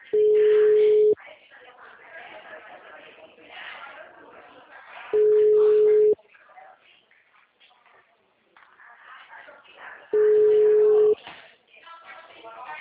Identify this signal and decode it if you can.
Ringback tone (Europe, 425 Hz), with speech.